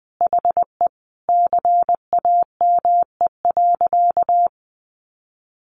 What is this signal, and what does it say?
Morse code, heard as a keyed tone, about 20 wpm, 700 Hz. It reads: HE CAME.
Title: Sentences from Top 1500 Words 20wpm